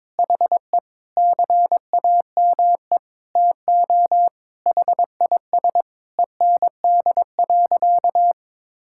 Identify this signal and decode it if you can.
Morse code, heard as a keyed tone, about 22 wpm, 700 Hz. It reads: HE CAME TO HIS END.
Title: Sentences from Top 1000 Words - Review Part 5 - 22wpm